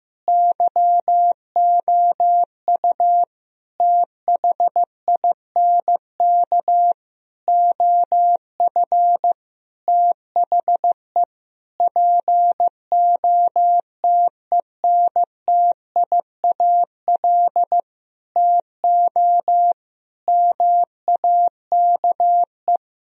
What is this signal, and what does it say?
Morse code, heard as a keyed tone, about 16 wpm, 700 Hz. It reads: YOU THINK OF THE POTENTIAL TO MAKE
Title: Sentences from Top 1000 Words - Review Part 4 - 15wpm